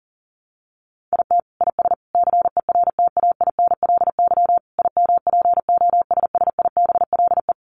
Morse code, heard as a keyed tone, about 40 wpm, 700 Hz. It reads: IT IS CERTAINLY IMPOSSIBLE
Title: Sentences from Top 1100 Words 40wpm